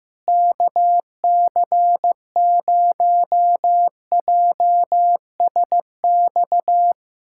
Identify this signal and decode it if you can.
Morse code, heard as a keyed tone, about 15 wpm, 700 Hz. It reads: KC0JSX
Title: USA Callsign DE Callsign 15wpm